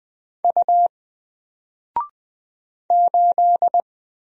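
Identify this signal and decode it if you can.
Morse code, heard as a keyed tone, about 20 wpm, 700 Hz. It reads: U 8
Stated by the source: Single letter and number Morse drill at 20 wpm, from N4LCD.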